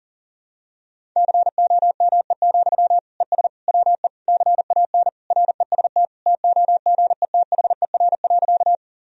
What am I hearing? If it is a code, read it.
Morse code, heard as a keyed tone, about 40 wpm, 700 Hz. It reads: COME, ES WE CAN REST TOGETHER.
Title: Sentences from Top 500 Words and Abbreviations 40wpm